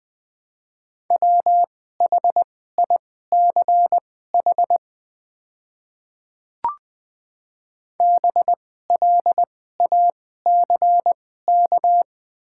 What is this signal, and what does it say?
Morse code, heard as a keyed tone, about 20 wpm, spaced as for 14 wpm, 700 Hz. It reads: WHICH BLACK
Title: 2x Character Spacing - Top 500 Words - 20wpm